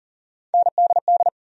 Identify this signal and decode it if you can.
Morse code, heard as a keyed tone, about 40 wpm, 700 Hz. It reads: NDD
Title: USA Call Sign Postfixes - 3 Letters 40wpm